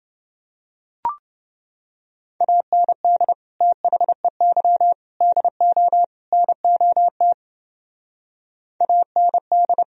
Morse code, heard as a keyed tone, about 29 wpm, 700 Hz. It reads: AND THEY DO NOT AND